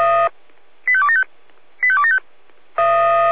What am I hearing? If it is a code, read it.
Five-tone selective call (ZVEI), heard as quick runs of five tones, tones of about 70 ms. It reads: 85275 85275